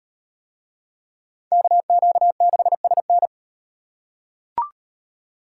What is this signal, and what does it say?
Morse code, heard as a keyed tone, about 34 wpm, 700 Hz. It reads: KQ6SN